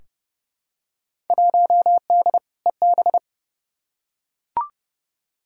Morse code, heard as a keyed tone, about 30 wpm, 700 Hz. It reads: 1D EB
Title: Field Day Exchanges - 30wpm